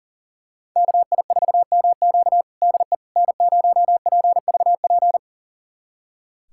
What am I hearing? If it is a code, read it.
Morse code, heard as a keyed tone, about 37 wpm, 700 Hz. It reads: KI4MQ DE N0PVP
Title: USA Callsign DE Callsign - 5 Characters - 40wpm